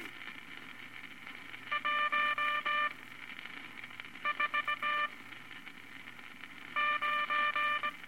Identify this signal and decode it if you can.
Morse code, heard as a keyed tone, about 18 wpm, 1190 Hz, with noise. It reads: 1 4 9
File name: Morse Code Records/15A LCM 158 Units I-V Groups of Five